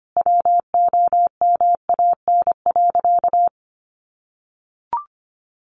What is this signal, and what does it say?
Morse code, heard as a keyed tone, about 25 wpm, 700 Hz. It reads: WOMAN.
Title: Sentences from Top 1100 Words 25wpm